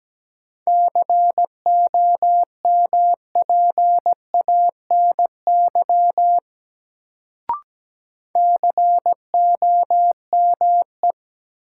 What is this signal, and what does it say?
Morse code, heard as a keyed tone, about 17 wpm, 700 Hz. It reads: COMPANY COME
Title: Sets of 2 Words - New Word Drill - 17wpm